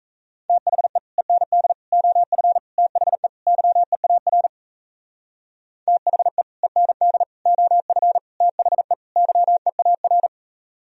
Morse code, heard as a keyed tone, about 36 wpm, 700 Hz. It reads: THE END OF THE YEAR THE END OF THE YEAR